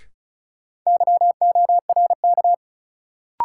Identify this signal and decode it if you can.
Morse code, heard as a keyed tone, about 34 wpm, 700 Hz. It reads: YORK
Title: Sets of 2 Words - New Word Drill - 35wpm